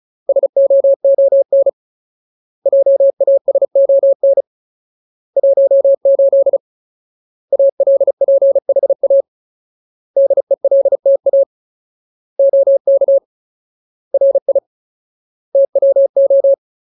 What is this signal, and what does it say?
Morse code, heard as a keyed tone, about 34 wpm, 550 Hz. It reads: SOON JASON 18 ALPHA DELTA OK RI TWO
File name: Ultimate Word List / 35WPM Ultimate